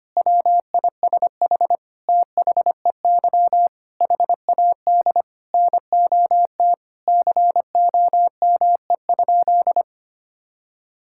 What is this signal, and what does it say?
Morse code, heard as a keyed tone, about 25 wpm, 700 Hz. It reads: WISH THEY HAD NOT COME?
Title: Sentences from Top 1600 Words 25wpm